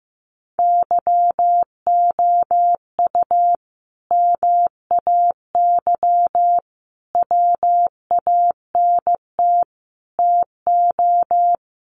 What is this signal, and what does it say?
Morse code, heard as a keyed tone, about 14 wpm, 700 Hz. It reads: YOU MAY WANT TO
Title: Sentences from Top 700 Words 15wpm